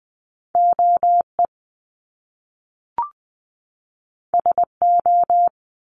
Morse code, heard as a keyed tone, about 20 wpm, 700 Hz. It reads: OE SO
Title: Top Two Letter Combinations in Words 20wpm